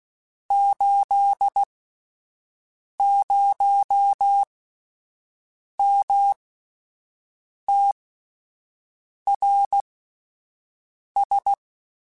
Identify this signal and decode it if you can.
Morse code, heard as a keyed tone, about 16 wpm, spaced as for 5 wpm, 800 Hz. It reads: 80MTRS
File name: QSOs / QSOs-mp3s / WD0FBT 1@10